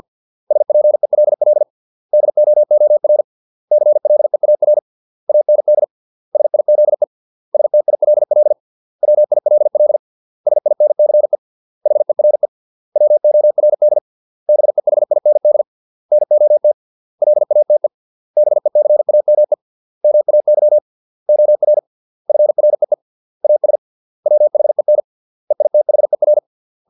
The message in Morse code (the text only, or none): SPELL DOOR CLEAR AND SIZE STILL WILL SINCE HERE WORD BEHIND NOT LATE BECAME MAY OR FREE AS WHEN EITHER